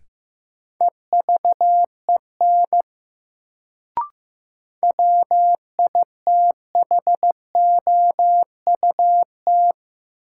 Morse code, heard as a keyed tone, about 15 wpm, 700 Hz. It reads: EVEN WITHOUT
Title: Sentences from Top 600 Words - New Word Drill - 15wpm